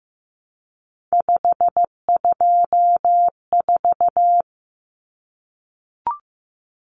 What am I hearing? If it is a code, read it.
Morse code, heard as a keyed tone, about 15 wpm, 700 Hz. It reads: 524
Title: Common Cut Numbers - 3-Digit 15wpm